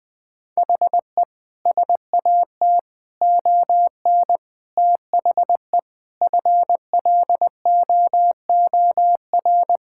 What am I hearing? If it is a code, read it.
Morse code, heard as a keyed tone, about 21 wpm, 700 Hz. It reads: HE SAT ON THE FLOOR
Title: Sentences with 5-Word Phrases - Spoken Beginning - 20wpm